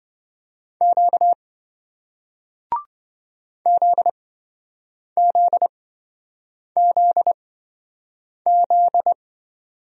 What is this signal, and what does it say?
Morse code, heard as a keyed tone, about 27 wpm, spaced as for 7 wpm, 700 Hz. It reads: Q ZZZZ